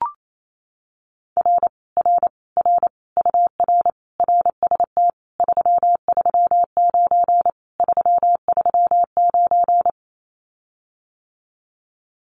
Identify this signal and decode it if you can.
Morse code, heard as a keyed tone, about 28 wpm, 700 Hz. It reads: R R R UR RST 339 339
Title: Common QSO Phrases Encore 28wpm